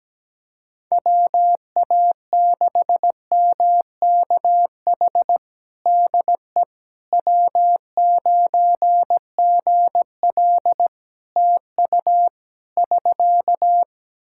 Morse code, heard as a keyed tone, about 17 wpm, 700 Hz. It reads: WA6MKH DE W9GL TU <SK>
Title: Practice QSO - Exchange 4 - Ending - 17wpm